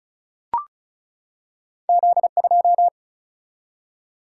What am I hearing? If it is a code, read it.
Morse code, heard as a keyed tone, about 36 wpm, 700 Hz. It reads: Z2
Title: Single-Letter Single-Number 35wpm